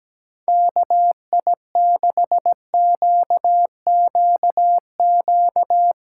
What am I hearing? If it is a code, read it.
Morse code, heard as a keyed tone, about 17 wpm, 700 Hz. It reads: KI6QQQ
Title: USA Callsign DE Callsign 17wpm